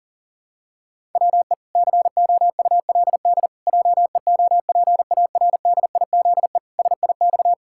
Morse code, heard as a keyed tone, about 40 wpm, 700 Hz. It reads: WE COULD JEOPARDIZE SIX
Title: Pangram Challenge v2 - 40wpm